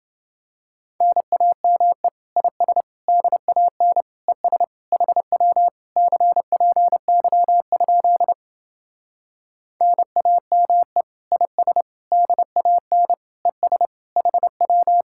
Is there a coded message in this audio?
Yes: Morse code from 1.0 to 15.0 s, heard as a keyed tone, about 28 wpm, 700 Hz. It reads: NAME IS DAN ES HW CPY? NAME IS DAN ES HW